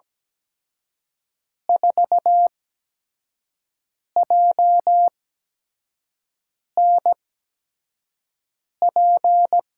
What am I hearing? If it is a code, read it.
Morse code, heard as a keyed tone, about 17 wpm, 700 Hz. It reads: 4 J N P
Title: USA Call Signs - 8x Character Spacing - 17wpm